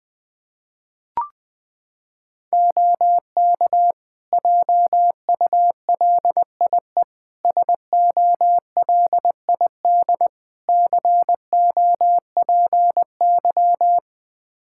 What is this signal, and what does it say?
Morse code, heard as a keyed tone, about 21 wpm, 700 Hz. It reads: OK JULIE SOLID COPY
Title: Practice QSO - Exchange 3 - More Info - 20wpm